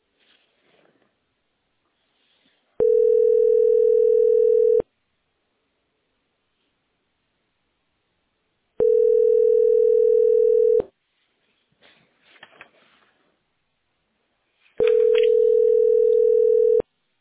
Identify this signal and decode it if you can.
Ringback tone (North America), 440 + 480 Hz.
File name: 092cdd33-09ec-4146-8ac7- a7d